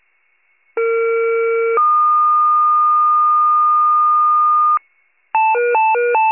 Two-tone page, heard as a one-second tone, then a three-second tone, about 460 then 1220 Hz. It is heard from 0.8 to 4.8 s.